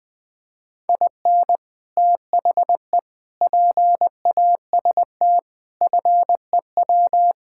Morse code, heard as a keyed tone, about 20 wpm, 700 Hz. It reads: IN THE PAST FEW
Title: Sentences with 5-Word Phrases - Spoken Beginning - 20wpm